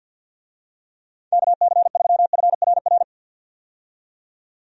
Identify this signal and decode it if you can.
Morse code, heard as a keyed tone, about 46 wpm, 700 Hz. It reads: KC3FRR
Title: USA Call Signs - 6 Characters - 50wpm